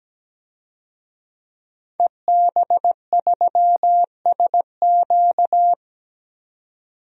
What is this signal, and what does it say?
Morse code, heard as a keyed tone, about 17 wpm, 700 Hz. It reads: EB3SQ
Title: DXCC-100 Call Signs - 5 Characters - 17wpm